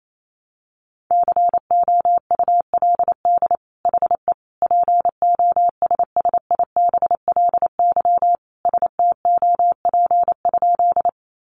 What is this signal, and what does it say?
Morse code, heard as a keyed tone, about 28 wpm, 700 Hz. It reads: COULD HE POSSIBLY STOP?
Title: Sentences from Top 1300 Words 28wpm